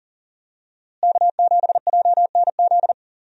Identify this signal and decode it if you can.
Morse code, heard as a keyed tone, about 40 wpm, 700 Hz. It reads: K7JNZ
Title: USA Call Signs - 5 Characters - Repeated 3x - 40wpm